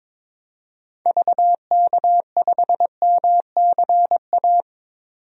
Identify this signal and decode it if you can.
Morse code, heard as a keyed tone, about 22 wpm, 700 Hz. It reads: VK5MCA
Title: DXCC-100 Call Signs - 6 Characters - 22wpm